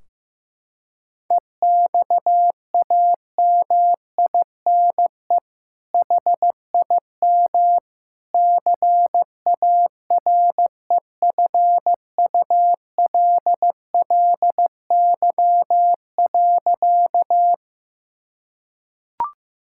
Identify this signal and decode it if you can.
Morse code, heard as a keyed tone, about 15 wpm, 700 Hz. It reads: EXAMINE HIM CAREFULLY.
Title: Sentences from Top 1600 Words 15wpm